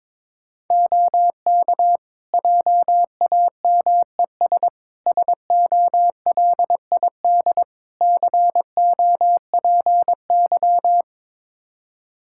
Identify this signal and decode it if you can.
Morse code, heard as a keyed tone, about 22 wpm, 700 Hz. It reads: OK JAMES SOLID COPY
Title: Practice QSO - Exchange 3 - More Info - 22wpm